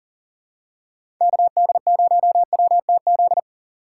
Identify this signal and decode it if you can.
Morse code, heard as a keyed tone, about 40 wpm, 700 Hz. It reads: KD0WTZ
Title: USA Call Signs - 3x Character Spacing - 40wpm